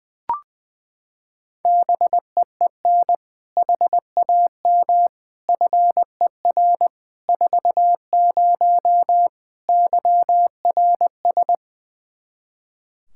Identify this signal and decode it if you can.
Morse code, heard as a keyed tone, about 20 wpm, 700 Hz. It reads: BEEN HAM FER 40 YRS
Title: Practice QSO - Exchange 3 - More Info - 20wpm